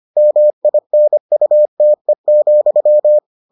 Morse code, heard as a keyed tone, about 25 wpm, 600 Hz. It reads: MINUTE,